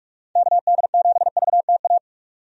Morse code, heard as a keyed tone, about 41 wpm, 700 Hz. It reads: KD7VTA
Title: USA Call Signs - 6 Characters - 45wpm